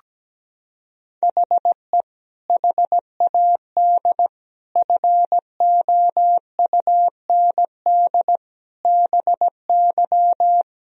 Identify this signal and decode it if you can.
Morse code, heard as a keyed tone, about 17 wpm, 700 Hz. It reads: HE HAD FOUND BY
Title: Sentences from Top 1000 Words - Review Part 3 - 17wpm